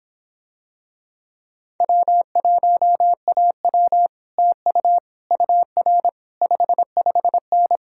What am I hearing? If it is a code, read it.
Morse code, heard as a keyed tone, about 26 wpm, 700 Hz. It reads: W1AW TU UR 55N